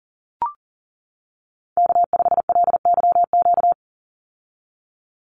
Morse code, heard as a keyed tone, about 40 wpm, 700 Hz. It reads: K5LYQ
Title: USA Call Signs 40wpm